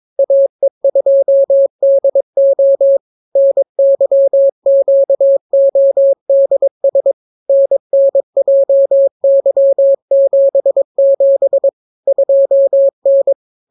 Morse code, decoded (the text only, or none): AE2DO NYQODS NNJY77 2N